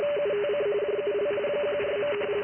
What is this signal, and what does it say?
RTTY, about 45.45 baud. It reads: UTC JUL 14.
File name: IC-7100